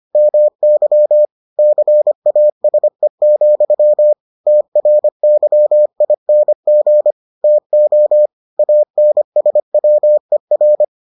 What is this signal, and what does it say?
Morse code, heard as a keyed tone, about 25 wpm, 600 Hz. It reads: MY CASE, TRYING TO ANSWER